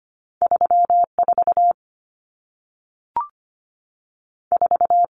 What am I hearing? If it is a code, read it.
Morse code, heard as a keyed tone, about 25 wpm, 700 Hz. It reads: 34 4
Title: Numbers - 2-Digit 25wpm